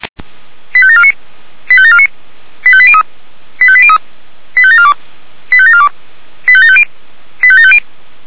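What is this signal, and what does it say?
Five-tone selective call (ZVEI), heard as quick runs of five tones, tones of about 70 ms. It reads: 85749 85749 85803 85803 85742 85742 85750 85750
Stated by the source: ZVEI five-tone call received on SDR by the feuerwehr-orlamuende fire brigade.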